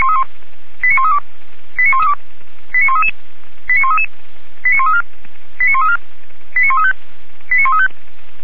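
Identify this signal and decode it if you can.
Five-tone selective call (ZVEI), heard as quick runs of five tones, tones of about 70 ms. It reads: 89132 89133 89133 89134 89134 89135 89135
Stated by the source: ZVEI five-tone selective call, address 89115, received on an SDR.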